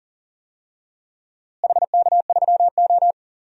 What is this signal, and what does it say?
Morse code, heard as a keyed tone, about 37 wpm, 700 Hz. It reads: HK3O
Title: DXCC-100 Call Signs - 4 Characters - Repeated 2x - 40wpm